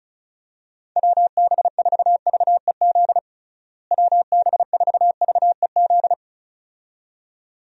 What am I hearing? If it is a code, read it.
Morse code, heard as a keyed tone, about 36 wpm, 700 Hz. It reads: WB4VEZ WB4VEZ
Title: USA Call Signs - 6 Characters - Repeated 2x - 35wpm